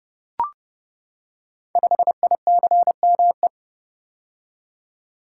Morse code, heard as a keyed tone, about 30 wpm, 700 Hz. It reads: 5ICME
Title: Code Groups - 5 Letter:Numbers - 30wpm